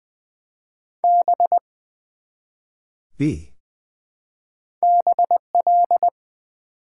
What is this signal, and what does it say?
Morse code, heard as a keyed tone, about 20 wpm, 700 Hz. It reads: B BL
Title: Build A Word - 1 Letter At A Time - 3 to 5-Letter Words - 20wpm